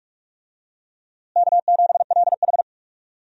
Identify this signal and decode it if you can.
Morse code, heard as a keyed tone, about 46 wpm, 700 Hz. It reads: K7LH
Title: USA Call Signs - 4 Characters - Repeated 2x - 45wpm